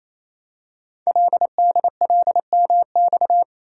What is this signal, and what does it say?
Morse code, heard as a keyed tone, about 29 wpm, 700 Hz. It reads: LDLMX
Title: Code Groups - 5 Letters - 28wpm